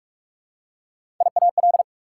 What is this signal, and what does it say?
Morse code, heard as a keyed tone, about 44 wpm, 690 Hz. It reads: IAL
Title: Prefixes and Suffixes 45wpm